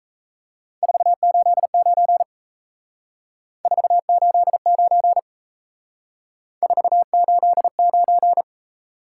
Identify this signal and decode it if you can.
Morse code, heard as a keyed tone, about 35 wpm, 700 Hz. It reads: 489 489 489
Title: Speed-Racing Top 180 QSO Elements 28wpm